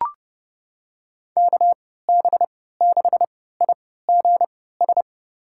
Morse code, heard as a keyed tone, about 32 wpm, spaced as for 17 wpm, 700 Hz. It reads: KB6IGS
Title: USA Call Signs - 6 Characters - 3x Character Spacing - 30wpm